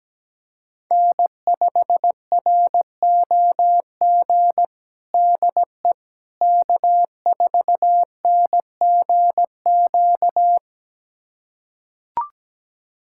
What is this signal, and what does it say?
Morse code, heard as a keyed tone, about 17 wpm, 700 Hz. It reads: N5ROG DE K4NGQ
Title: USA Callsign DE Callsign - 5 Characters - 17wpm